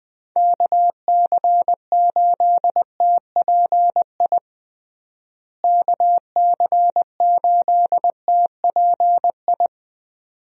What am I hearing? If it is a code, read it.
Morse code, heard as a keyed tone, about 20 wpm, 700 Hz. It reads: KC8TPI KC8TPI
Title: USA Call Signs - 6 Characters - Repeated 2x - 20wpm